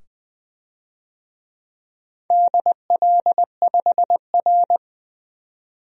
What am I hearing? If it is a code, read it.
Morse code, heard as a keyed tone, about 21 wpm, 700 Hz. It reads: DL5R
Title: DXCC-100 Call Signs - 4 Characters - 20wpm